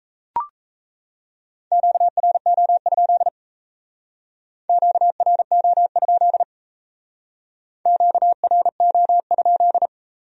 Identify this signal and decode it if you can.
Morse code, heard as a keyed tone, about 35 wpm, 700 Hz. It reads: QRO? QRO? QRO?